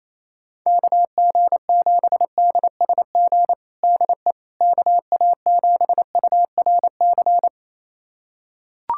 Morse code, heard as a keyed tone, about 28 wpm, 700 Hz. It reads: KG7DSG DE KA7URC
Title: USA Callsign DE Callsign - 6 Characters - 28wpm